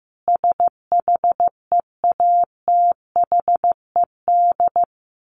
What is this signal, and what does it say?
Morse code, heard as a keyed tone, about 15 wpm, 700 Hz. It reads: SHEATHED
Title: A Sound Of Thunder 15wpm